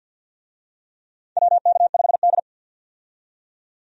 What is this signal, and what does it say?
Morse code, heard as a keyed tone, about 52 wpm, 700 Hz. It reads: WK5D
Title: USA Call Signs - 4 Characters - 50wpm